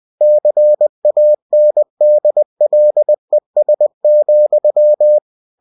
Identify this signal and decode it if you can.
Morse code, heard as a keyed tone, about 20 wpm, 600 Hz. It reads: CANDLES,
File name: cw-books / winnie-the-pooh / winnie-20wpm-page-0048